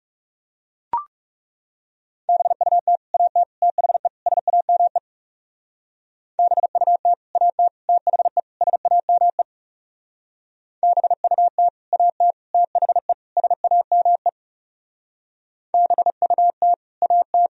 Morse code, heard as a keyed tone, about 38 wpm, 700 Hz. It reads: BUT AT THE SAME BUT AT THE SAME BUT AT THE SAME BUT AT